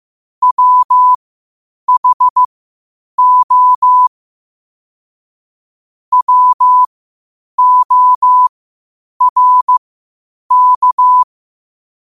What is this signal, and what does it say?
Morse code, heard as a keyed tone, about 15 wpm, spaced as for 8 wpm, 1000 Hz. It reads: WHO WORK
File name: qotdfiles / 8wpm